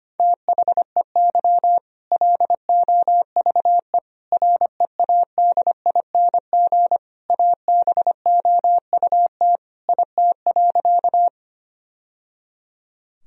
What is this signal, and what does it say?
Morse code, heard as a keyed tone, about 25 wpm, 700 Hz. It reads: THEY LOVE READING ABOUT IT.
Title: Sentences from Top 1000 Words - Review Part 3 - 25wpm